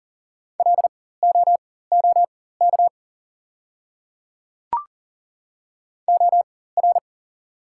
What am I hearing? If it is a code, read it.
Morse code, heard as a keyed tone, about 40 wpm, spaced as for 19 wpm, 690 Hz. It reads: LOOK OR
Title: 4x Character Spacing - Top 100 Words - 40wpm